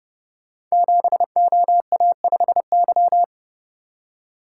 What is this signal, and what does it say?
Morse code, heard as a keyed tone, about 30 wpm, 700 Hz. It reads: 7OA5Y
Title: Code Groups - 5 Letter:Numbers - 30wpm